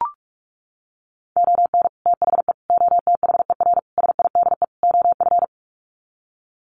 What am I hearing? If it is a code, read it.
Morse code, heard as a keyed tone, about 44 wpm, 700 Hz. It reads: ON THE OTHER SIDE OF
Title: Sets of 5 Words 45wpm